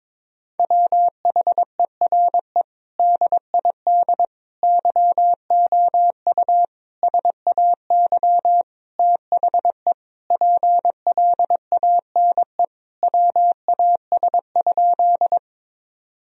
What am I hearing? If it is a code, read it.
Morse code, heard as a keyed tone, about 22 wpm, 700 Hz. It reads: WHERE DID YOU SAY THE PLANE WAS?
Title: Sentences from Top 1000 Words - Review Part 3 - 22wpm